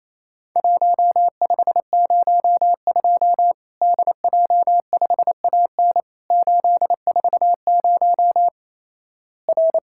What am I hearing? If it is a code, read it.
Morse code, heard as a keyed tone, about 28 wpm, 700 Hz. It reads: 1502 DJ5AN 840 R
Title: Worked All Europe DX Contest - QTC Exchange - 28wpm